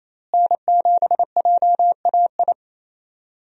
Morse code, heard as a keyed tone, about 28 wpm, 700 Hz. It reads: N7JAI
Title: USA Call Signs - 5 Characters - Repeated 3x - 28wpm